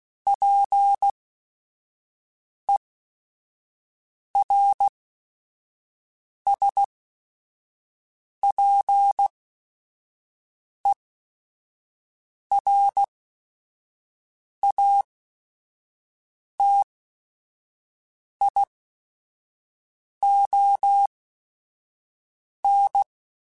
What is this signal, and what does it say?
Morse code, heard as a keyed tone, about 16 wpm, spaced as for 5 wpm, 800 Hz. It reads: PERSPERATION
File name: Quotes / Quotes-mp3s / Persperation @5